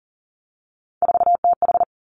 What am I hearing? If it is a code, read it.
Morse code, heard as a keyed tone, about 37 wpm, 700 Hz. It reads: 4TH
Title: Three Challenging Letter-Number Combinations 40wpm